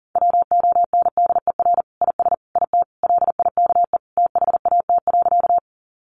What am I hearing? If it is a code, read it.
Morse code, heard as a keyed tone, about 40 wpm, 700 Hz. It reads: WONDER IS IT LIKE THAT.
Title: Sentences from Top 500 Words 40wpm